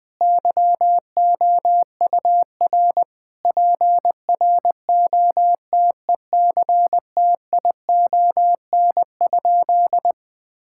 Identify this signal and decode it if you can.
Morse code, heard as a keyed tone, about 20 wpm, 700 Hz. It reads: YOUR PROTECTION?
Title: Sentences from Top 1600 Words 20wpm